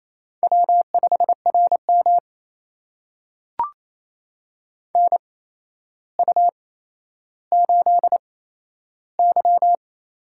Morse code, heard as a keyed tone, about 28 wpm, spaced as for 12 wpm, 700 Hz. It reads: W5RM NU8Y
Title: USA Call Signs - 4 Characters - 8x Character Spacing - 28wpm